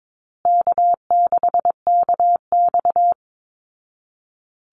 Morse code, heard as a keyed tone, about 22 wpm, 700 Hz. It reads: K6KX
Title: Easy Call Signs Repeated 3x - 22wpm